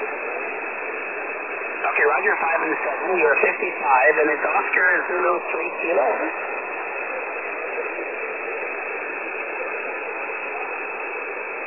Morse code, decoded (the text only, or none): none